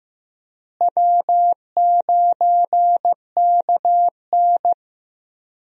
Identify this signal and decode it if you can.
Morse code, heard as a keyed tone, about 15 wpm, 700 Hz. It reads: W9KN
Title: USA Call Signs - 4 Characters - Repeated 2x - 15wpm